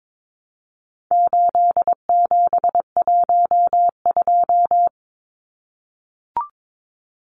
Morse code, heard as a keyed tone, about 22 wpm, 700 Hz. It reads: 8712
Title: Numbers - 4-Digit 22wpm